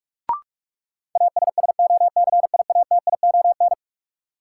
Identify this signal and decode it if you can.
Morse code, heard as a keyed tone, about 42 wpm, 700 Hz. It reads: ASSOCIATION
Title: Sentences from Top 1100 Words - New Word Drill Alt - 45wpm